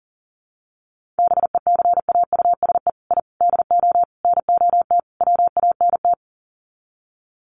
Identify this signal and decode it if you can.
Morse code, heard as a keyed tone, about 40 wpm, 700 Hz. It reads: BECAUSE I DO NOT WANT
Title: Sets of 5 Words Encore 40wpm